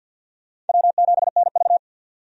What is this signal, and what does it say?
Morse code, heard as a keyed tone, about 51 wpm, 700 Hz. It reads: W7NV
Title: USA Call Signs - 4 Characters - 50wpm